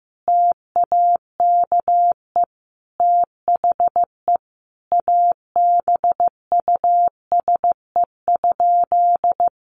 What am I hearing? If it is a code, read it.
Morse code, heard as a keyed tone, about 14 wpm, 700 Hz. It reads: TAKE THE ABUSE?
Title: Sentences from Top 1300 Words 15wpm